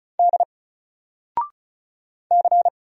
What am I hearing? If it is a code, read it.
Morse code, heard as a keyed tone, about 36 wpm, 700 Hz. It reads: D C
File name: Single-Letter-Number-35wpm